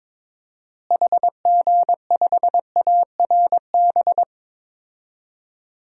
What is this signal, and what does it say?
Morse code, heard as a keyed tone, about 22 wpm, 700 Hz. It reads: HG5ARB
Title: DXCC-100 Call Signs - 6 Characters - Repeated 2x - 22wpm